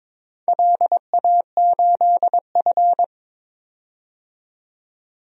Morse code, heard as a keyed tone, about 22 wpm, 700 Hz. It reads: LA8F
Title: DXCC-100 Call Signs - 4 Characters - 22wpm